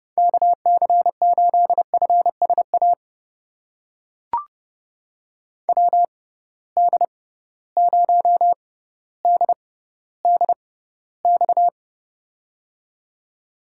Morse code, heard as a keyed tone, about 30 wpm, spaced as for 15 wpm, 700 Hz. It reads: KC8FSA WD0DDX
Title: USA Call Signs - 6 Characters - 6x Character Spacing - 30wpm